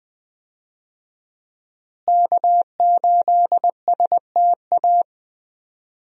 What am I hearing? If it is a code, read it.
Morse code, heard as a keyed tone, about 20 wpm, 700 Hz. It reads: K8STA
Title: USA Call Signs - 5 Characters - Repeated 2x - 20wpm